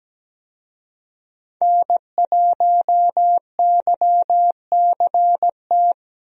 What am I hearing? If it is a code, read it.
Morse code, heard as a keyed tone, about 17 wpm, 700 Hz. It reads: N1YCT
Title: USA Call Signs - 5 Characters - Repeated 3x - 17wpm